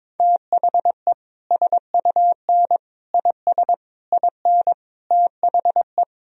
Morse code, heard as a keyed tone, about 22 wpm, 700 Hz. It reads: THE SUN IS IN THE
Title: Sentences from Top 1000 Words - Review Part 3 - 22wpm